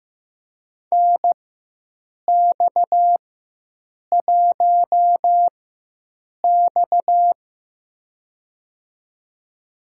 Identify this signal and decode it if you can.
Morse code, heard as a keyed tone, about 15 wpm, spaced as for 7 wpm, 700 Hz. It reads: NX1X
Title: USA Call Signs - 4x Character Spacing - 15wpm